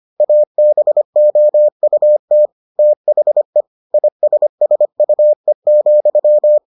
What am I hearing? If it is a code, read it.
Morse code, heard as a keyed tone, about 25 wpm, 600 Hz. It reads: ABOUT THE ISSUE,